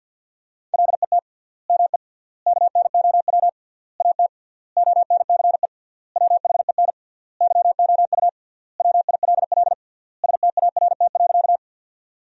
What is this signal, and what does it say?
Morse code, heard as a keyed tone, about 50 wpm, 700 Hz. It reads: LET ME KNOW AT ONCE WHEN YOU WILL START.
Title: 3x Word-Spacing - Sentences from Top 300 Words - 50wpm